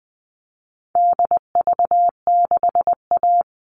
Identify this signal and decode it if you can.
Morse code, heard as a keyed tone, about 20 wpm, 700 Hz. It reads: DV6A
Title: All Cut Numbers - 4-Digit 20wpm